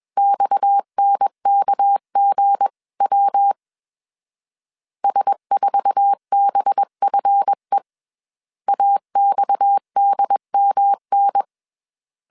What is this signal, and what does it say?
Morse code, heard as a keyed tone, about 21 wpm, 800 Hz. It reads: XNKGW S4BFE AXDMN
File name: CW / DET / 1 9 1500C S C 20wpm 2310